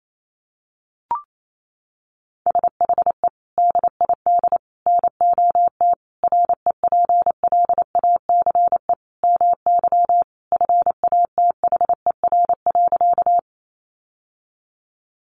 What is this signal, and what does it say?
Morse code, heard as a keyed tone, about 28 wpm, 700 Hz. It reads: SHE DID NOT REPLACE MY FATHER.
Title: Sentences from Top 1400 Words 28wpm